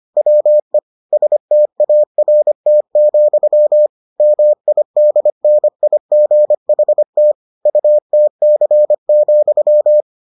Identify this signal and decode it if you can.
Morse code, heard as a keyed tone, about 25 wpm, 600 Hz. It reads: WE START, MIDNIGHT UTC,